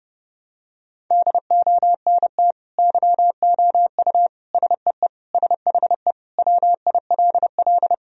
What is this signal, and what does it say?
Morse code, heard as a keyed tone, about 30 wpm, 700 Hz. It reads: DONT YOU SEE SHE WILL
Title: Sentences from Top 1000 Words - Review Part 2 - 30wpm